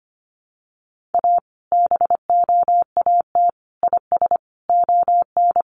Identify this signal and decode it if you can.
Morse code, heard as a keyed tone, about 25 wpm, 700 Hz. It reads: A BOAT IS ON
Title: Sentences from Top 1500 Words 25wpm